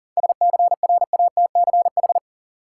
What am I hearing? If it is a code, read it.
Morse code, heard as a keyed tone, about 40 wpm, 700 Hz. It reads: SCRATCH
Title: Seven Letter Words - 40wpm